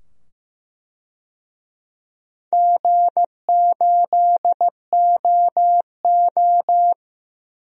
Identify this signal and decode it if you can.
Morse code, heard as a keyed tone, about 15 wpm, 700 Hz. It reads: G8OO
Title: DXCC-100 Call Signs - 4 Characters - 15wpm